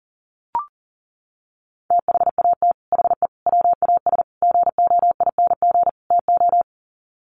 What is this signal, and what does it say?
Morse code, heard as a keyed tone, about 40 wpm, 700 Hz. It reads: THAT HE WAS GOING TO
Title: Sets of 5 Words 40wpm